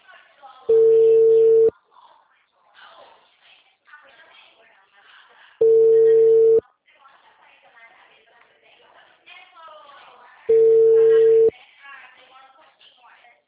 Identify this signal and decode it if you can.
Ringback tone (China, 450 Hz), with speech.